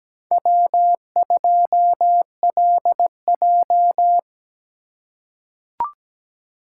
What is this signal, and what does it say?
Morse code, heard as a keyed tone, about 17 wpm, 700 Hz. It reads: W2LJ
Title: USA Call Signs - 4 Characters - 8x Character Spacing - 17wpm